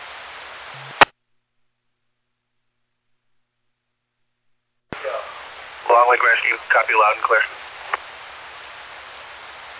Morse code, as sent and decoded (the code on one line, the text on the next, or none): none